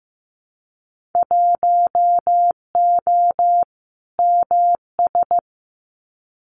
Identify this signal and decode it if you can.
Morse code, heard as a keyed tone, about 15 wpm, 700 Hz. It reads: 1O MS
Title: Winter Field Day Exchanges 15wpm